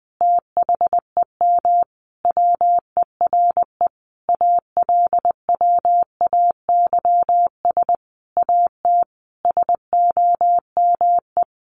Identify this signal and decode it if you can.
Morse code, heard as a keyed tone, about 20 wpm, 700 Hz. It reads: THEM WERE ALWAYS AT SOME
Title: Sentences from Top 1200 Words 20wpm